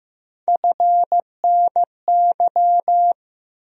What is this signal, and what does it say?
Morse code, heard as a keyed tone, about 15 wpm, 700 Hz. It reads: FNY
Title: USA Call Sign Postfixes - 3 Letters 15wpm